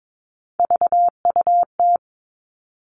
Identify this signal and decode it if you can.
Morse code, heard as a keyed tone, about 22 wpm, 700 Hz. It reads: VUT
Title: Three Challenging Letter-Number Combinations 22wpm